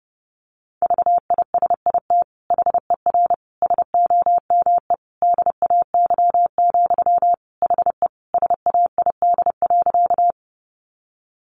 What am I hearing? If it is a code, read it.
Morse code, heard as a keyed tone, about 30 wpm, 700 Hz. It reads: VISIT HER SOME DAY, HE SAID.
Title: Sentences from Top 900 Words 30wpm